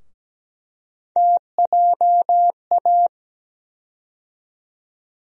Morse code, heard as a keyed tone, about 17 wpm, 700 Hz. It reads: TJA
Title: USA Call Sign Postfixes - 3 Letters 17wpm